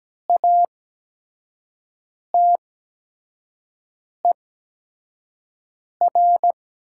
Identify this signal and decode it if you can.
Morse code, heard as a keyed tone, about 17 wpm, spaced as for 5 wpm, 700 Hz. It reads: ATER
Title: 8x Character Spacing - Top 200 Words - 17wpm